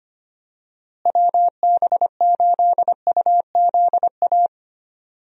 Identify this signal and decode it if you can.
Morse code, heard as a keyed tone, about 25 wpm, 700 Hz. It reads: WB8UZA
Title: USA Call Signs - 6 Characters - Repeated 2x - 25wpm